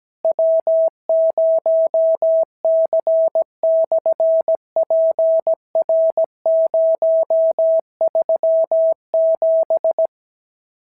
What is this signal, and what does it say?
Morse code, heard as a keyed tone, about 17 wpm, 650 Hz. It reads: W0C/PR037
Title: SOTA QSO Practice 17wpm